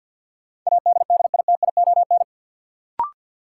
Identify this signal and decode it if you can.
Morse code, heard as a keyed tone, about 52 wpm, 700 Hz. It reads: ADDITION
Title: Sentences from Top 1500 Words - New Word Drill Alt - 50wpm